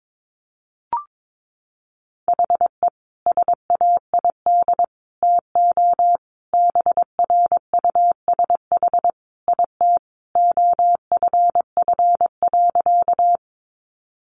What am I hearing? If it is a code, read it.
Morse code, heard as a keyed tone, about 22 wpm, 700 Hz. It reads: HE SAID TO BRUSH IT OFF.
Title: Sentences from Top 1100 Words 22wpm